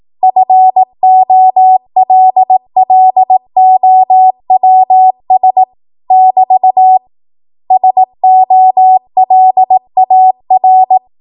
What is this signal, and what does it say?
Morse code, heard as a keyed tone, about 18 wpm, 750 Hz. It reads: FOLLOWS <BT> SOLAR
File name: Morse / 18WPM